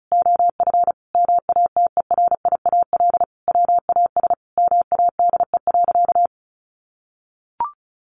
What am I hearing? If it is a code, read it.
Morse code, heard as a keyed tone, about 34 wpm, 700 Hz. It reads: OF MATERIAL WAS MADE.
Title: Sentences from Top 1200 Words 35wpm